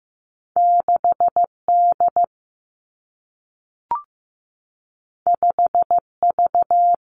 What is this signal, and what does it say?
Morse code, heard as a keyed tone, about 15 wpm, 700 Hz. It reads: 6D 5V
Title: Two Challenging Letter-Number Combinations 15wpm